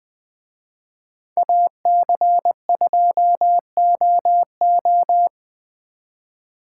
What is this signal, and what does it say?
Morse code, heard as a keyed tone, about 24 wpm, 700 Hz. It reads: AC2OO
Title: Speed Racing - USA 2x2 Call Signs 20wpm